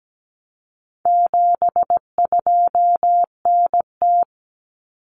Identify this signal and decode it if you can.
Morse code, heard as a keyed tone, about 17 wpm, 700 Hz. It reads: 72NT
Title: Common Cut Numbers - 4-Digit 17wpm